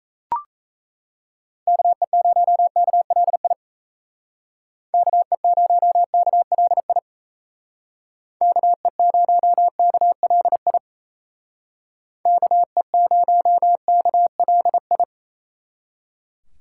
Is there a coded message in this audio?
Yes: Morse code from 1.7 to 15.0 s, heard as a keyed tone, about 36 wpm, 700 Hz. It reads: KE0KLI KE0KLI KE0KLI KE0KLI